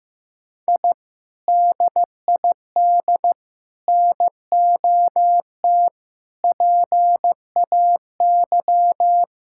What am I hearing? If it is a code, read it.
Morse code, heard as a keyed tone, about 15 wpm, 700 Hz. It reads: I DID NOT PAY
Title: Sentences from Top 1000 Words - Review Part 6 - 15wpm